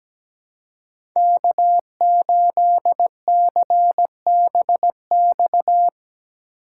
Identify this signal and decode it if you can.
Morse code, heard as a keyed tone, about 17 wpm, 700 Hz. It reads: K8CBX
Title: USA Call Signs - 5 Characters - 6x Character Spacing - 17wpm